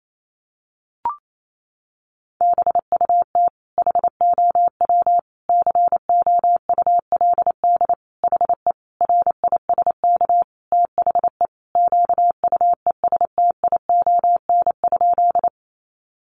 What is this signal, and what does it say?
Morse code, heard as a keyed tone, about 28 wpm, 700 Hz. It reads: BUT HOW COULD HE RISK THE QUESTION?
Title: Sentences from Top 600 Words 28wpm